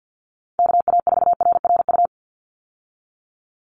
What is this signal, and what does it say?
Morse code, heard as a keyed tone, about 49 wpm, 700 Hz. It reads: KA4RRU
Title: Call Signs Repeated 3x - 50wpm